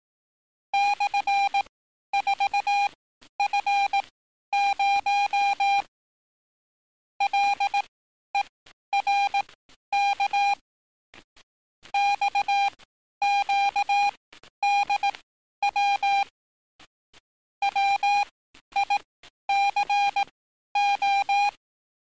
Morse code, heard as a keyed tone, about 18 wpm, spaced as for 11 wpm, 800 Hz. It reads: /4F0 LERK XQDW WICO
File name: training / NAVPERS 11049to11060 / NAVPERS 11051 RA RB